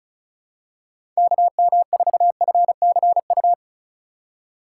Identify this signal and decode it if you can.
Morse code, heard as a keyed tone, about 36 wpm, 700 Hz. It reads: KM4FCU
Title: USA Call Signs - 6 Characters - 2x Character Spacing - 35wpm